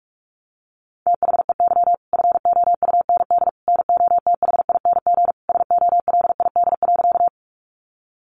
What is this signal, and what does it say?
Morse code, heard as a keyed tone, about 44 wpm, 700 Hz. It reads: THEY FOUND NOTHING SOLID.
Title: Sentences from Top 1500 Words 45wpm